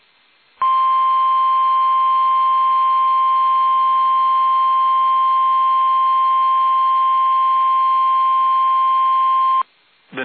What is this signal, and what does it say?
Emergency Alert System (SAME) tones (the 1050 Hz attention tone). It is heard from 1.0 to 9.4 s.
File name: eas / KUSO / KUSO